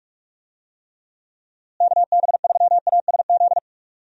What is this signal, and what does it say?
Morse code, heard as a keyed tone, about 42 wpm, 700 Hz. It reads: KB3ASZ
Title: USA Call Signs - 6 Characters - Repeated 2x - 45wpm